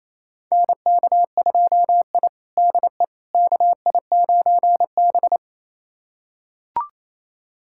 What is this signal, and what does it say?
Morse code, heard as a keyed tone, about 28 wpm, 700 Hz. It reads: NK2I DE KI9B
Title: USA Callsign DE Callsign - 4 Characters - 28wpm